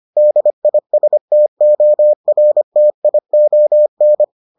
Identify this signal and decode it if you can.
Morse code, heard as a keyed tone, about 25 wpm, 600 Hz. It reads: DISTORTION